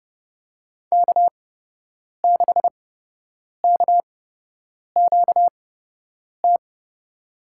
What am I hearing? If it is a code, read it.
Morse code, heard as a keyed tone, about 30 wpm, spaced as for 8 wpm, 700 Hz. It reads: K6KQT
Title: USA Call Signs - 8x Character Spacing - 30wpm